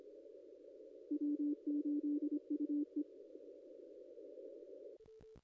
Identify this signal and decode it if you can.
Morse code, heard as a keyed tone, about 25 wpm, 290 Hz. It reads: W8UE
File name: contests / ss-cw-k5tr-op-n5rz -0200